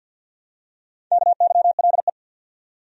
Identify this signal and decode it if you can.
Morse code, heard as a keyed tone, about 53 wpm, 700 Hz. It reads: KYLE
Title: USA Common Names - Max 4 Letters - 50wpm